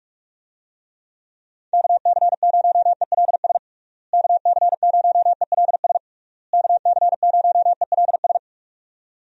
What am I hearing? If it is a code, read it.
Morse code, heard as a keyed tone, about 42 wpm, 700 Hz. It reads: KC0ELS KC0ELS KC0ELS
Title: USA Call Signs - 6 Characters - Repeated 3x - 45wpm